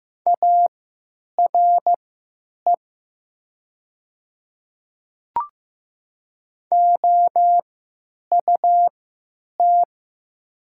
Morse code, heard as a keyed tone, about 15 wpm, spaced as for 9 wpm, 700 Hz. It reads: ARE OUT
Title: 3x Character Spacing - Top 100 Words - 15wpm